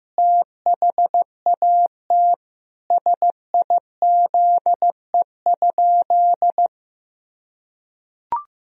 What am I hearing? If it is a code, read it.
Morse code, heard as a keyed tone, about 15 wpm, 700 Hz. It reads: THAT SIZE?
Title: Sentences from Top 1000 Words - Review Part 2 - 15wpm